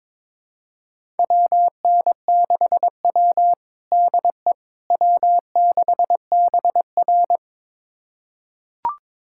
Morse code, heard as a keyed tone, about 23 wpm, 700 Hz. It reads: WN6W DE W6BR
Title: USA Callsign DE Callsign - 4 Characters - 22wpm